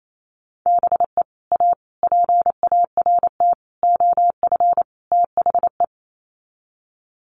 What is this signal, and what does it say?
Morse code, heard as a keyed tone, about 28 wpm, 700 Hz. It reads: BE A PART OF THE
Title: Sets of 5 Words Encore 28wpm